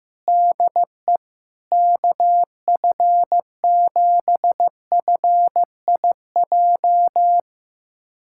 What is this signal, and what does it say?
Morse code, heard as a keyed tone, about 15 wpm, 700 Hz. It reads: DE KF7FIJ
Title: USA Callsign DE Callsign - 6 Characters - 15wpm